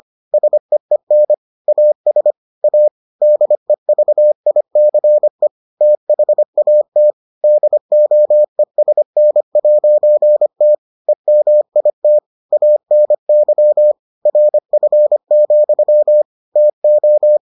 Morse code, heard as a keyed tone, about 25 wpm, 600 Hz. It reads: SEEN AS A DEVICE THAT DOESN'T EMIT ANY RF, TO